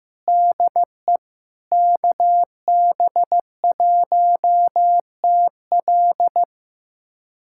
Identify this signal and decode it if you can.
Morse code, heard as a keyed tone, about 15 wpm, 700 Hz. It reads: DE KB1TL
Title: USA Callsign DE Callsign - 5 Characters - 15wpm